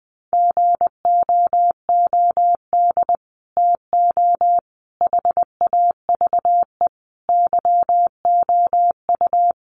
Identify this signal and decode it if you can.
Morse code, heard as a keyed tone, about 20 wpm, 700 Hz. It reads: GOOD TO HAVE YOU
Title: Sets of 5 Words Encore2 20wpm